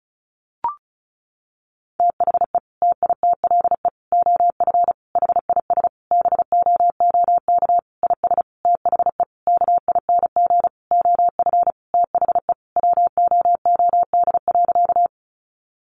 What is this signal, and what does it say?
Morse code, heard as a keyed tone, about 34 wpm, 700 Hz. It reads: THE TITLE OF HIS BOOK IS THE KING OF THE WOOD.
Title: Sentences from Top 1200 Words 35wpm